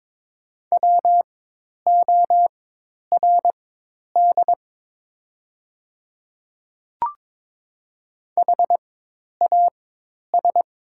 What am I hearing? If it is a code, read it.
Morse code, heard as a keyed tone, about 22 wpm, spaced as for 10 wpm, 700 Hz. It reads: WORD HAS
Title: 4x Character Spacing - Top 300 Words - 22wpm